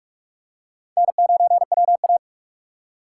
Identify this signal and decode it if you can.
Morse code, heard as a keyed tone, about 46 wpm, 700 Hz. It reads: N9WA
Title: USA Call Signs - 4 Characters - 6x Character Spacing - 45wpm